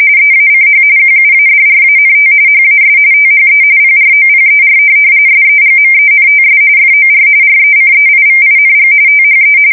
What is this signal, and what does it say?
RTTY, about 45.45 baud. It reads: EACH QUADRANT HAS THE PROPER SEGMENTS FOR RECEIVING THE
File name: fsk170 / printing-telegraph